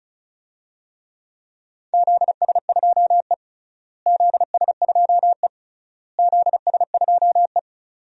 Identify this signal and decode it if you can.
Morse code, heard as a keyed tone, about 36 wpm, 700 Hz. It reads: ZS2E ZS2E ZS2E
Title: DXCC-100 Call Signs - 4 Characters - Repeated 3x - 35wpm